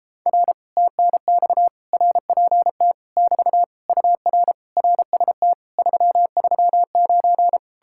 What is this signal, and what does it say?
Morse code, heard as a keyed tone, about 37 wpm, 700 Hz. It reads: R TNX RPT <BT> UR RST 339